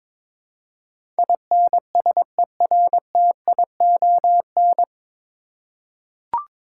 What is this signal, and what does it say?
Morse code, heard as a keyed tone, about 22 wpm, 700 Hz. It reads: INSERTION
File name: 9 Letter Words - First 3 Letters Given - 22wpm